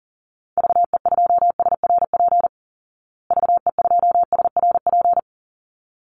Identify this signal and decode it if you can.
Morse code, heard as a keyed tone, about 40 wpm, 700 Hz. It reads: VE2SRP VE2SRP
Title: Call Signs Repeated 3x - 40wpm